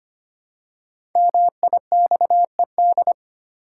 Morse code, heard as a keyed tone, about 25 wpm, 700 Hz. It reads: MIXED
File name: 5 Letter Words - First 3 Letters Given - 25wpm